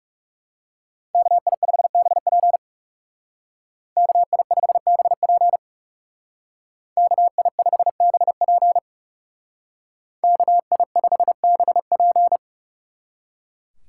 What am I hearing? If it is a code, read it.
Morse code, heard as a keyed tone, about 37 wpm, 700 Hz. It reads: KI5BP KI5BP KI5BP KI5BP
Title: Speed Racing - USA 2x2 Call Signs 30wpm